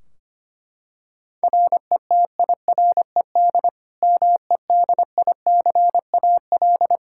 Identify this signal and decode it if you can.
Morse code, heard as a keyed tone, about 25 wpm, 700 Hz. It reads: RETIRED MEDICAL
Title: Practice QSO - Exchange 3 - More Info - 25wpm